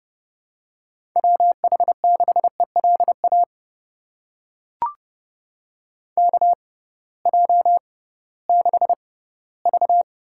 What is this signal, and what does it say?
Morse code, heard as a keyed tone, about 30 wpm, spaced as for 17 wpm, 700 Hz. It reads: WH6ELA KJ6V
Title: USA Call Signs - 6 Characters - 6x Character Spacing - 30wpm